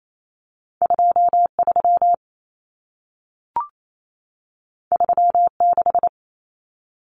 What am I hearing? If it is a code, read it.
Morse code, heard as a keyed tone, about 28 wpm, 700 Hz. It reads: 23 36
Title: Numbers - 2-Digit 28wpm